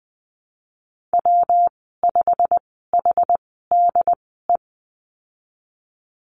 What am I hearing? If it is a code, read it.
Morse code, heard as a keyed tone, about 20 wpm, spaced as for 14 wpm, 700 Hz. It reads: W5HDE
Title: 128 Lesson 21g - Number 6 - Call Signs 20wpm